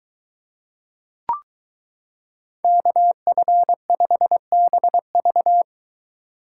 Morse code, heard as a keyed tone, about 23 wpm, 700 Hz. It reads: KF5BV